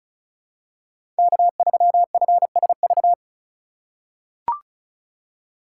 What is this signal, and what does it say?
Morse code, heard as a keyed tone, about 34 wpm, 700 Hz. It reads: K3FSV
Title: USA Call Signs - 2x Character Spacing - 35wpm